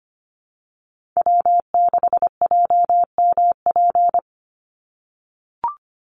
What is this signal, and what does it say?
Morse code, heard as a keyed tone, about 25 wpm, 700 Hz. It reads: W6JMP
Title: Call Signs - 25wpm